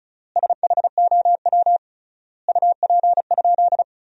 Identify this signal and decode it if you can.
Morse code, heard as a keyed tone, about 34 wpm, 700 Hz. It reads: SHOW UP?
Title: 3x Word-Spacing - Sentences from Top 200 Words - 35wpm